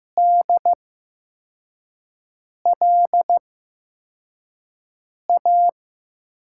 Morse code, heard as a keyed tone, about 15 wpm, spaced as for 4 wpm, 700 Hz. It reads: DLA
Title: USA Call Signs - 6 Characters - 8x Character Spacing - 15wpm